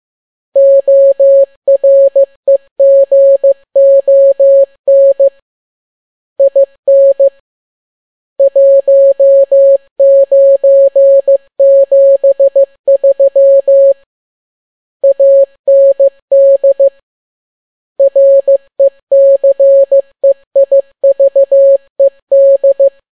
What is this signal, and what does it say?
Morse code, heard as a keyed tone, about 15 wpm, 550 Hz. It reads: OREGON IN 1973 AND RECEIVED